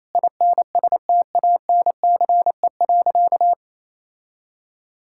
Morse code, heard as a keyed tone, about 28 wpm, 700 Hz. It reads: INSTANCE.
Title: Sentences from Top 1000 Words - Review Part 5 - 28wpm